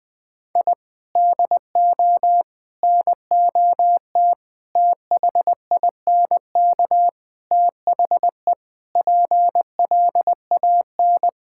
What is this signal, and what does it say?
Morse code, heard as a keyed tone, about 20 wpm, 700 Hz. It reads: I DO NOT THINK THE PLAN
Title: Sentences with 5-Word Phrases - 20wpm